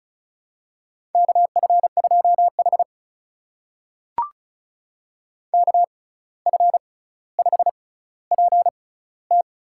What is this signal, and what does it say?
Morse code, heard as a keyed tone, about 36 wpm, spaced as for 17 wpm, 700 Hz. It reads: KF2H KF5PT
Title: USA Call Signs - 6x Character Spacing - 35wpm